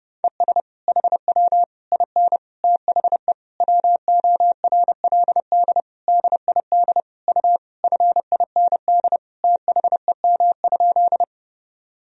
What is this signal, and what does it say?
Morse code, heard as a keyed tone, about 32 wpm, 700 Hz. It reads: ES HW IN THE WORLD DID U FIND THEM?
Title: Sentences from Top 200 Words and Abbreviations 30wpm